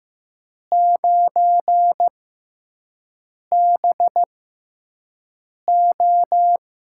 Morse code, heard as a keyed tone, about 15 wpm, spaced as for 5 wpm, 700 Hz. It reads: 9BO
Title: USA Call Signs - 6 Characters - 6x Character Spacing - 15wpm